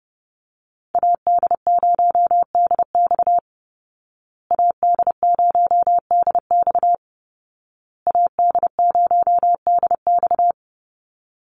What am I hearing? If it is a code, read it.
Morse code, heard as a keyed tone, about 30 wpm, 700 Hz. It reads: AD0DX AD0DX AD0DX
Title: Call Signs Repeated 2x - 30wpm